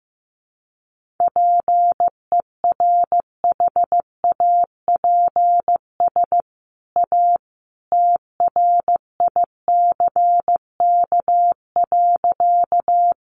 Morse code, heard as a keyed tone, about 15 wpm, 700 Hz. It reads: PERHAPS A TRICK.
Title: Sentences from Top 1500 Words 15wpm